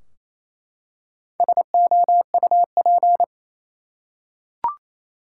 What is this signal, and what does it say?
Morse code, heard as a keyed tone, about 28 wpm, 700 Hz. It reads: SOUP
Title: Four Letter Words - 28wpm